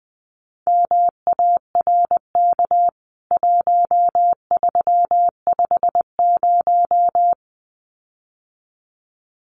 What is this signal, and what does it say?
Morse code, heard as a keyed tone, about 19 wpm, 700 Hz. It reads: MARK 1350
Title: CWT Exchange 20wpm